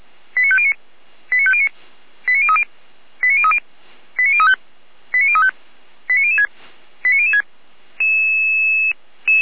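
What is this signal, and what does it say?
Five-tone selective call (ZVEI), heard as quick runs of five tones, tones of about 70 ms. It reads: 89509 89509 89030 89030 89035 89035 89006 89006
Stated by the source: ZVEI five-tone call 89001 received on SDR, Feuerwehr Orlamünde.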